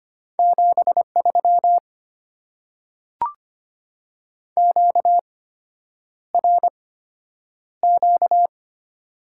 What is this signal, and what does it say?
Morse code, heard as a keyed tone, about 25 wpm, spaced as for 9 wpm, 700 Hz. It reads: 73 QRQ
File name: Top 90 QSO Elements - 8x Character Spacing - 25wpm